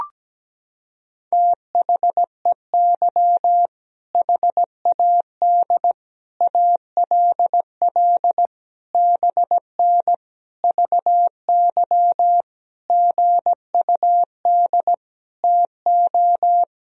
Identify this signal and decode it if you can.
Morse code, heard as a keyed tone, about 17 wpm, 700 Hz. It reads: THEY HAD ALL BN VY GUD TO
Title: Sentences from Top 200 Words and Abbreviations 17wpm